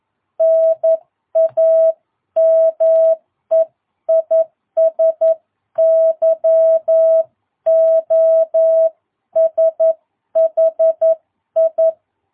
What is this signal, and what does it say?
Morse code, heard as a keyed tone, about 11 wpm, 660 Hz. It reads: NAME IS YOSHI
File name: jh8cbh / 2022cw